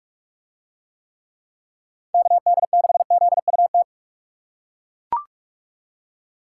Morse code, heard as a keyed tone, about 41 wpm, 700 Hz. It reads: KD6ZUT